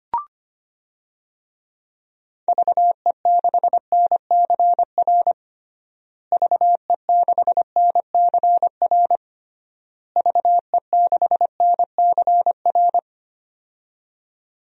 Morse code, heard as a keyed tone, about 25 wpm, 700 Hz. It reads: VE6NCR VE6NCR VE6NCR
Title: DXCC-100 Call Signs - 6 Characters - Repeated 3x - 25wpm